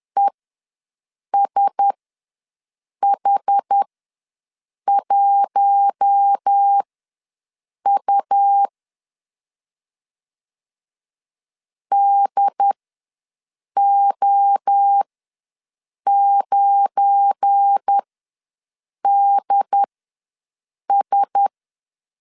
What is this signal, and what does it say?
Morse code, heard as a keyed tone, about 10 wpm, spaced as for 6 wpm, 800 Hz. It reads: ESH1U DO9DS
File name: LEZIONI / 7 GF9Z / 7 GF9 ws 10wpm 4034